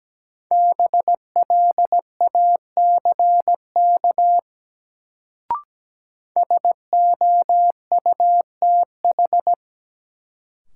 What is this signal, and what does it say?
Morse code, heard as a keyed tone, about 17 wpm, 700 Hz. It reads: BLACK SOUTH
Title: Sentences from Top 400 Words - New Word Drill - 17wpm